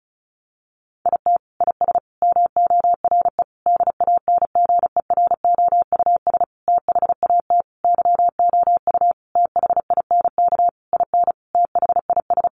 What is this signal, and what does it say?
Morse code, heard as a keyed tone, about 34 wpm, 700 Hz. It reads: IT IS MORE DANGEROUS THAT YOU THINK IN THIS
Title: Sentences from Top 1200 Words 35wpm